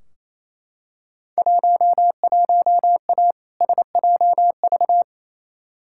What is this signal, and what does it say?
Morse code, heard as a keyed tone, about 28 wpm, 700 Hz. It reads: 11A SJV
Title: Field Day Exchanges Encore - 28wpm